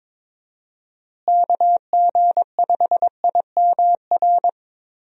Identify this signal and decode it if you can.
Morse code, heard as a keyed tone, about 22 wpm, 700 Hz. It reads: KG5IMR
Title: USA Call Signs - 6 Characters - Repeated 2x - 22wpm